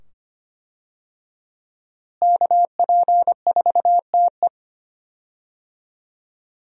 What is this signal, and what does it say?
Morse code, heard as a keyed tone, about 25 wpm, 700 Hz. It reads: KP4TE
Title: DXCC-100 Call Signs - 5 Characters - Repeated 2x - 25wpm